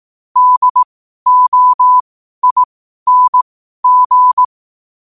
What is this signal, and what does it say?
Morse code, heard as a keyed tone, about 18 wpm, spaced as for 12 wpm, 1000 Hz. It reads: DOING
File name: qotdfiles / 12wpm